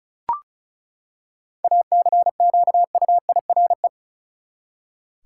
Morse code, heard as a keyed tone, about 34 wpm, 700 Hz. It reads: ACQUIRE
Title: Seven Letter Words - 35wpm